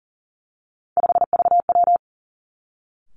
Morse code, heard as a keyed tone, about 38 wpm, 700 Hz. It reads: 5VW
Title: Three Challenging Letter-Number Combinations 40wpm